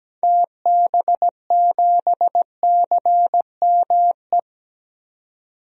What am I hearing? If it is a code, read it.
Morse code, heard as a keyed tone, about 17 wpm, 700 Hz. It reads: TB7CME
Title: DXCC-100 Call Signs - 6 Characters - 17wpm